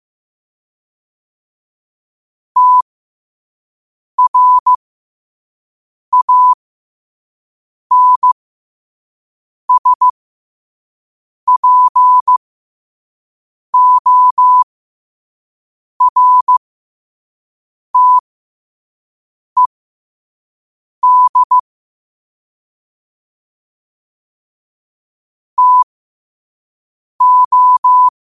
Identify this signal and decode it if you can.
Morse code, heard as a keyed tone, about 15 wpm, spaced as for 5 wpm, 1000 Hz. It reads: TRANSPORTED TO
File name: qotdfiles / 5wpm